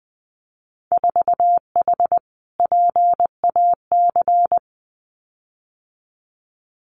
Morse code, heard as a keyed tone, about 20 wpm, 700 Hz. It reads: 4H PAC
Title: Winter Field Day Exchanges Encore 20wpm